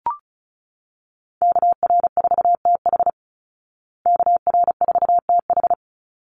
Morse code, heard as a keyed tone, about 34 wpm, 700 Hz. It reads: KR4TH KR4TH
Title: Call Signs Repeated 3x - 35wpm